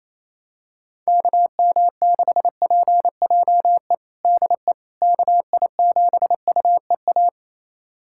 Morse code, heard as a keyed tone, about 28 wpm, 700 Hz. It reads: KM6PJE DE KI7UEA
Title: USA Callsign DE Callsign 28wpm